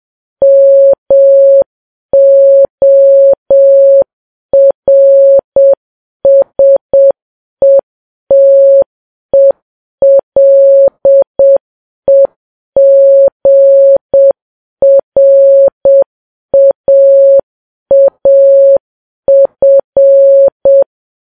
Morse code, heard as a keyed tone, about 7 wpm, 550 Hz. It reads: MORSETELEGRAAF
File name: Files Web tls / Morse on9fv / morse7kps